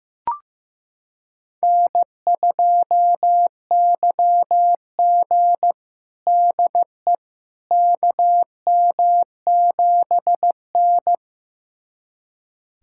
Morse code, heard as a keyed tone, about 15 wpm, 700 Hz. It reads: N2YG DE KM7N
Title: USA Callsign DE Callsign - 4 Characters - 15wpm